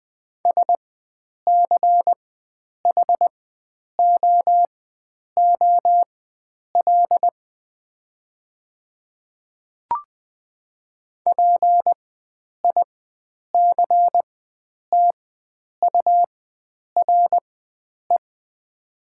Morse code, heard as a keyed tone, about 20 wpm, spaced as for 9 wpm, 700 Hz. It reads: SCHOOL PICTURE
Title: 4x Character Spacing - Top 500 Words - 20wpm